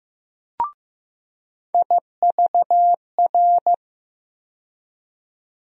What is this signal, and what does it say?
Morse code, heard as a keyed tone, about 15 wpm, 700 Hz. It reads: IVR
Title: USA Call Sign Postfixes - 3 Letters 15wpm